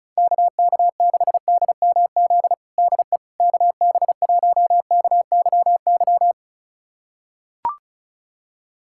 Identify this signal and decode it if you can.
Morse code, heard as a keyed tone, about 34 wpm, 700 Hz. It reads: KK6DMZ DE KB1KYY
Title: USA Callsign DE Callsign - 6 Characters - 35wpm